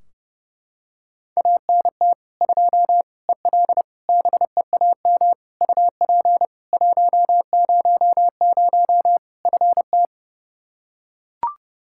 Morse code, heard as a keyed tone, about 32 wpm, 700 Hz. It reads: ANT 2 EL BEAM UP 100 FT
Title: Practice QSO - Exchange 2 - Info - 30wpm